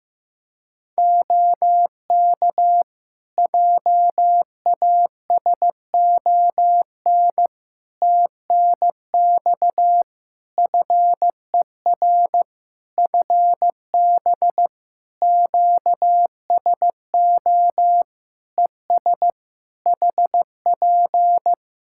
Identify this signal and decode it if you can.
Morse code, heard as a keyed tone, about 15 wpm, 700 Hz. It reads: OK JASON TNX FER FB QSO ES HP
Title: Practice QSO - Exchange 4 - Ending - 15wpm